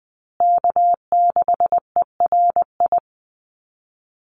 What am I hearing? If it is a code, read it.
Morse code, heard as a keyed tone, about 20 wpm, 700 Hz. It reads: K6ERI
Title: USA Call Signs Repeated 3x 20wpm